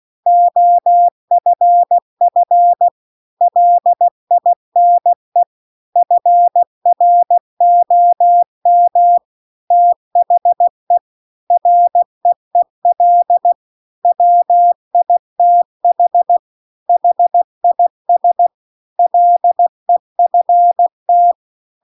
Morse code, heard as a keyed tone, about 16 wpm, 700 Hz. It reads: OFF LINE FROM THE REEL WITH HIS LEFT
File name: CW-The-Big-Twoheaded-River-2-16WPM